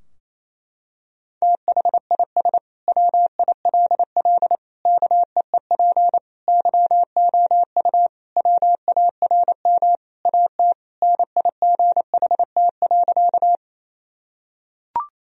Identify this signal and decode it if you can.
Morse code, heard as a keyed tone, about 27 wpm, 700 Hz. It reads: THIS WILL KEEP YOU WARM AT NIGHT.
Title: Sentences from Top 1000 Words - Review Part 2 - 28wpm